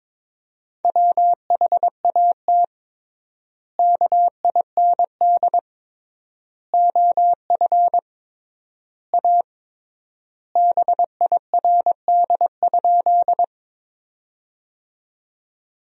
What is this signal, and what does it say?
Morse code, heard as a keyed tone, about 22 wpm, 700 Hz. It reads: WHAT KIND OF A BIRD?
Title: 3x Word-Spacing - Sentences from Top 400 Words - 22wpm